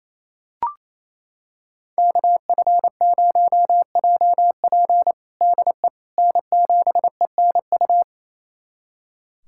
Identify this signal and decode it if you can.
Morse code, heard as a keyed tone, about 28 wpm, 700 Hz. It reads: KF0JP DE N7ENU
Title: USA Callsign DE Callsign - 5 Characters - 28wpm